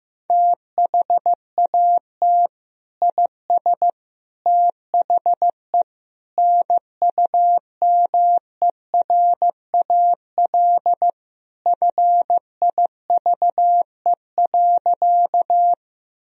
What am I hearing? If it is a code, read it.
Morse code, heard as a keyed tone, about 15 wpm, 700 Hz. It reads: THAT IS THE NUMERAL FIVE.
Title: Sentences from Top 1000 Words - Review Part 2 - 15wpm